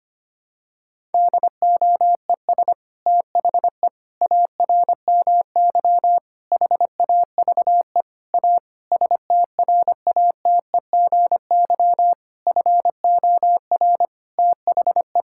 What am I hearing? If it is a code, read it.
Morse code, heard as a keyed tone, about 25 wpm, 700 Hz. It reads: DOES THE ARMY HAVE A STRATEGY FOR THE
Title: Sentences from Top 1000 Words - Review Part 5 - 25wpm